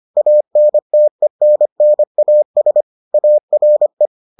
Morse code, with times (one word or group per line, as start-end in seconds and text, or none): 0.2-2.8 ANTENNAS
3.1-4.1 ARE